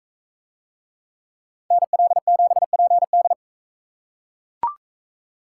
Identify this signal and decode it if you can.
Morse code, heard as a keyed tone, about 39 wpm, 700 Hz. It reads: NL7PD